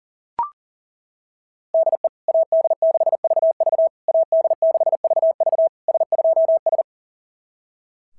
Morse code, heard as a keyed tone, about 37 wpm, 690 Hz. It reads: DE AD6VV AD6VV S2S
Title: SOTA QSO Practice 40wpm